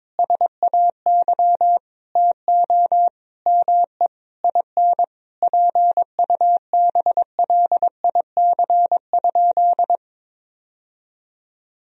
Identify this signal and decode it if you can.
Morse code, heard as a keyed tone, about 21 wpm, 700 Hz. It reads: SAY TO ME IN PUBLIC?
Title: Sentences from Top 1000 Words - Review Part 3 - 22wpm